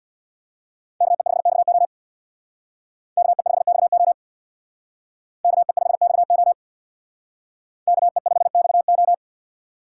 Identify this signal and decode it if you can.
Morse code, heard as a keyed tone, about 62 wpm, 700 Hz. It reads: KE5XO KE5XO KE5XO KE5XO
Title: Speed Racing - USA 2x2 Call Signs 50wpm